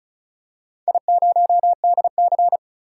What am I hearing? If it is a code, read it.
Morse code, heard as a keyed tone, about 34 wpm, 700 Hz. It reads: I0DC
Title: DXCC-100 Call Signs - 4 Characters - 35wpm